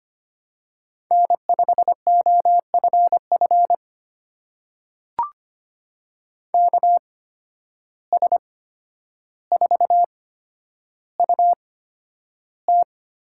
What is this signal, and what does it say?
Morse code, heard as a keyed tone, about 25 wpm, spaced as for 11 wpm, 700 Hz. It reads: N5OFF KS4UT
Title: USA Call Signs - 5 Characters - 8x Character Spacing - 25wpm